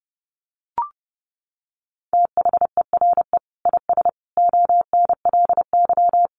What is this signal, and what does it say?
Morse code, heard as a keyed tone, about 30 wpm, 700 Hz. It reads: THERE IS ONLY
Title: Sentences from Top 300 Words 30wpm